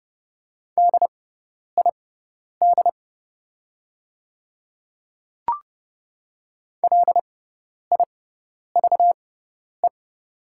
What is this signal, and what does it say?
Morse code, heard as a keyed tone, about 29 wpm, spaced as for 10 wpm, 700 Hz. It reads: DID LIVE
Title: 6x Character Spacing - Top 300 Words - 30wpm